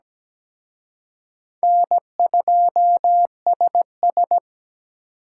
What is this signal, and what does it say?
Morse code, heard as a keyed tone, about 20 wpm, 700 Hz. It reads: N2SS
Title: Speed Racing - USA 1x2 Call Signs 17wpm